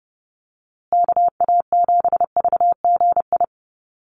Morse code, heard as a keyed tone, about 30 wpm, 700 Hz. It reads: KA7VGI
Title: USA Call Signs Repeated 2x 30wpm